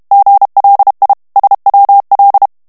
Morse code, heard as a keyed tone, about 32 wpm, 800 Hz. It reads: GLI SWL
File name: texte clair 2 vitesse 30 mots minute